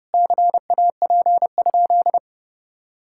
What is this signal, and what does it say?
Morse code, heard as a keyed tone, about 30 wpm, 700 Hz. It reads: CAP?
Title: Sentences from Top 1600 Words 30wpm